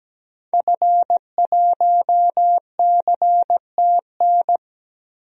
Morse code, heard as a keyed tone, about 17 wpm, 700 Hz. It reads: F1CTN
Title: DXCC-100 Call Signs - 5 Characters - 17wpm